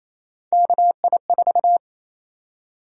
Morse code, heard as a keyed tone, about 28 wpm, 700 Hz. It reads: KI4
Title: USA Call Sign Prefixes - 28wpm